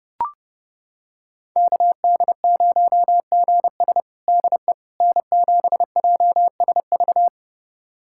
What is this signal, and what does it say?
Morse code, heard as a keyed tone, about 32 wpm, 700 Hz. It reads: KD0GS DE N7JSV
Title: USA Callsign DE Callsign - 5 Characters - 30wpm